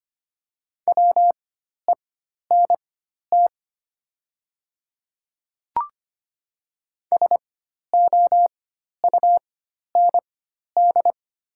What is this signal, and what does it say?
Morse code, heard as a keyed tone, about 25 wpm, spaced as for 12 wpm, 700 Hz. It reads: WENT SOUND
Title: 4x Character Spacing - Top 300 Words - 25wpm